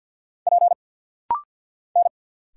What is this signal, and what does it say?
Morse code, heard as a keyed tone, about 50 wpm, 700 Hz. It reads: P N
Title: Single Letter-Number Alt - Rapid-Fire 50wpm